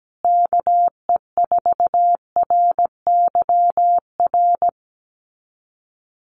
Morse code, heard as a keyed tone, about 17 wpm, 700 Hz. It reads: KE4RYR
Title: USA Call Signs Repeated 3x 17wpm